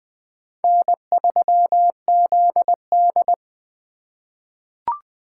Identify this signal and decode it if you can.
Morse code, heard as a keyed tone, about 20 wpm, 700 Hz. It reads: N3ZD
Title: USA Call Signs - 3x Character Spacing - 20wpm